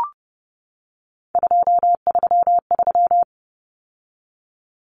Morse code, heard as a keyed tone, about 30 wpm, 700 Hz. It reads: 233
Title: Numbers - 3-Digit 30wpm